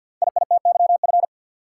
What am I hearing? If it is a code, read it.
Morse code, heard as a keyed tone, about 49 wpm, 700 Hz. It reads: IITYF
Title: Code Groups - 5 Letters - 50wpm